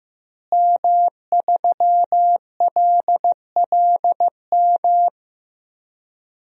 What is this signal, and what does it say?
Morse code, heard as a keyed tone, about 15 wpm, 700 Hz. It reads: M3LLM
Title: DXCC-100 Call Signs - 5 Characters - Repeated 2x - 15wpm